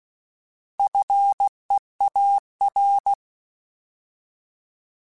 Morse code, heard as a keyed tone, about 16 wpm, 800 Hz. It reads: FEAR
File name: Alphabetical / Alphabetical mp3s / Fs @15